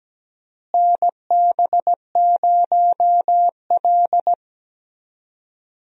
Morse code, heard as a keyed tone, about 17 wpm, 700 Hz. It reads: NB0L
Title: USA Call Signs - 4 Characters - Repeated 2x - 17wpm